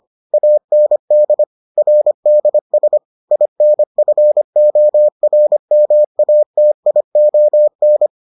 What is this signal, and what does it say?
Morse code, heard as a keyed tone, about 25 wpm, 600 Hz. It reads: AND RDS INFORMATION